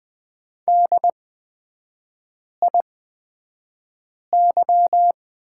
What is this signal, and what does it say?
Morse code, heard as a keyed tone, about 20 wpm, spaced as for 5 wpm, 700 Hz. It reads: DIY
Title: 250 Lesson 40e - BK - Three Characters 20wpm